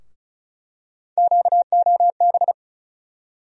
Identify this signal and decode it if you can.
Morse code, heard as a keyed tone, about 34 wpm, 700 Hz. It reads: QOB
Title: USA Call Sign Postfixes - 3 Letters 35wpm